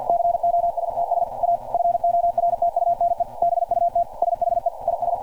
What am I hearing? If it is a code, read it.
Morse code, heard as a keyed tone, about 50 wpm, 690 Hz. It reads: COSTS A FORTUNE BUT SHE IS
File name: archived / CRSnet